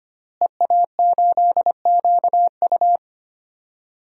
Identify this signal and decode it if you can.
Morse code, heard as a keyed tone, about 25 wpm, 700 Hz. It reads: EA8QU
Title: DXCC-100 Call Signs - 5 Characters - Repeated 2x - 25wpm